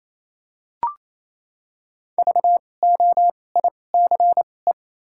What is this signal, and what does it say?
Morse code, heard as a keyed tone, about 28 wpm, spaced as for 20 wpm, 700 Hz. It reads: VOICE
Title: 2x Character Spacing - Top 500 Words - 28wpm